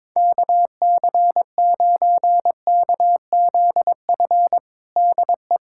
Morse code, heard as a keyed tone, about 22 wpm, 700 Hz. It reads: KC9KZF DE
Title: USA Callsign DE Callsign - 6 Characters - 22wpm